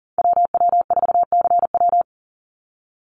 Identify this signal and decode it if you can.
Morse code, heard as a keyed tone, about 40 wpm, 700 Hz. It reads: WW4CW
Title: USA Call Signs 40wpm